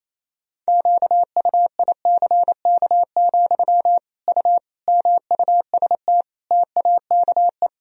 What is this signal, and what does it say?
Morse code, heard as a keyed tone, about 28 wpm, 700 Hz. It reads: QUICK, U MUST TAKE
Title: Sentences from Top 500 Words and Abbreviations 28wpm